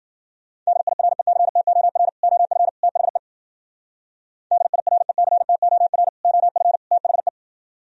Morse code, heard as a keyed tone, about 51 wpm, 700 Hz. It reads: DIRECTOR OF THE DIRECTOR OF THE